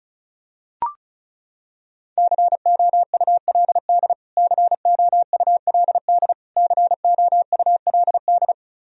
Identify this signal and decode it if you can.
Morse code, heard as a keyed tone, about 36 wpm, 700 Hz. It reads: COULD COULD COULD
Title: Top 100 Words Repeated 35wpm